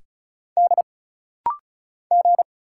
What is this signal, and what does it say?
Morse code, heard as a keyed tone, about 34 wpm, 700 Hz. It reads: D G
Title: Single Letters - Rapid-Fire 35wpm